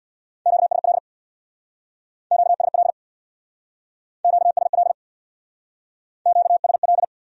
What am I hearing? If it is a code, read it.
Morse code, heard as a keyed tone, about 60 wpm, 700 Hz. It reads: QSL QSL QSL QSL
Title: Speed-Racing Q-Codes - 50wpm